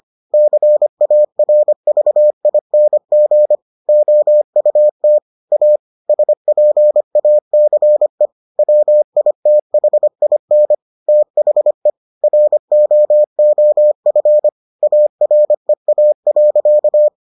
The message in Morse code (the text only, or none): CARVING OUT A SPACE WITHIN THE ROOF AREA.